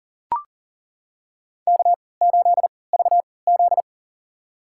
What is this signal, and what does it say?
Morse code, heard as a keyed tone, about 40 wpm, spaced as for 23 wpm, 700 Hz. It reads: K8VZ
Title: USA Call Signs - 4 Characters - 3x Character Spacing - 40wpm